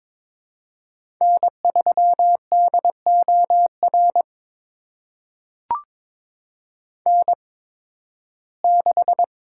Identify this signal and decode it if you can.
Morse code, heard as a keyed tone, about 21 wpm, spaced as for 14 wpm, 700 Hz. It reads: N3DOR N6
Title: USA Call Signs - 5 Characters - 8x Character Spacing - 22wpm